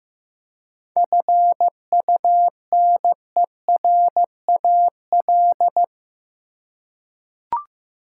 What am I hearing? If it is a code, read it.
Morse code, heard as a keyed tone, about 15 wpm, 700 Hz. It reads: FUNERAL
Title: Seven Letter Words - 15wpm